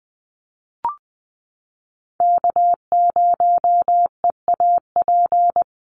Morse code, heard as a keyed tone, about 20 wpm, 690 Hz. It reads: K0EAP
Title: NA SOTA Chaser Callsigns 20wpm